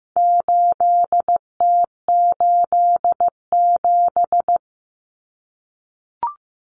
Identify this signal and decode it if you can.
Morse code, heard as a keyed tone, about 15 wpm, 700 Hz. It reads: 8T87
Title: Common Cut Numbers - 4-Digit 15wpm